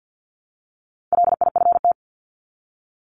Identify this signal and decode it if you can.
Morse code, heard as a keyed tone, about 50 wpm, 700 Hz. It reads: LIFT
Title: Top 751-1000 Words 50wpm